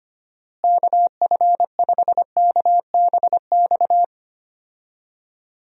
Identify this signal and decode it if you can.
Morse code, heard as a keyed tone, about 25 wpm, 700 Hz. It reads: KF5KBX
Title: USA Call Signs - 6 Characters - Repeated 2x - 25wpm